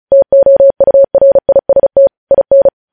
Morse code, heard as a keyed tone, about 36 wpm, 550 Hz. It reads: TOURIST IN
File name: podcast / News headlines-1x-35x35